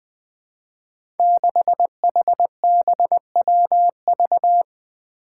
Morse code, heard as a keyed tone, about 20 wpm, 700 Hz. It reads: 6HBWV
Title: Five Challenging Letter-Number Combinations 20wpm